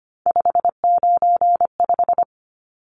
Morse code, heard as a keyed tone, about 25 wpm, 700 Hz. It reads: 595
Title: Numbers - 3-Digit 25wpm